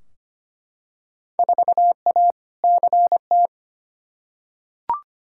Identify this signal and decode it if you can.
Morse code, heard as a keyed tone, about 25 wpm, 700 Hz. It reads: 4A CT
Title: Field Day Exchanges Encore - 25wpm